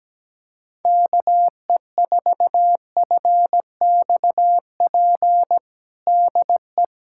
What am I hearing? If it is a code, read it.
Morse code, heard as a keyed tone, about 17 wpm, 700 Hz. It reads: KE4FXP DE
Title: USA Callsign DE Callsign - 6 Characters - 17wpm